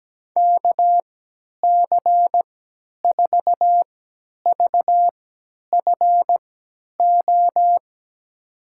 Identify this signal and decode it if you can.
Morse code, heard as a keyed tone, about 17 wpm, spaced as for 10 wpm, 700 Hz. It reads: KC4VFO
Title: USA Call Signs - 6 Characters - 3x Character Spacing - 17wpm